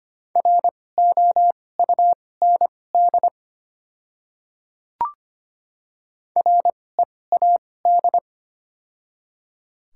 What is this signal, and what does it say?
Morse code, heard as a keyed tone, about 25 wpm, spaced as for 18 wpm, 700 Hz. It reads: ROUND READ
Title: 2x Character Spacing - Top 200 Words - 25wpm